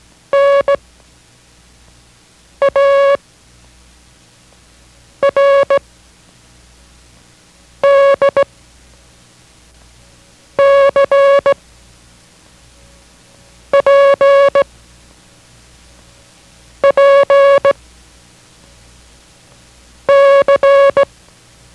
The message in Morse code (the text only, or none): N A R D C P P C